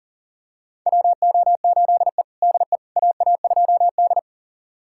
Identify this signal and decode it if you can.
Morse code, heard as a keyed tone, about 40 wpm, 700 Hz. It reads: WO8E DE AA2D
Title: USA Callsign DE Callsign - 4 Characters - 40wpm